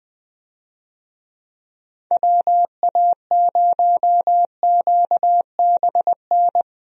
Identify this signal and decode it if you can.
Morse code, heard as a keyed tone, about 21 wpm, 700 Hz. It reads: WA0QBN
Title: USA Call Signs - 6 Characters - Repeated 3x - 20wpm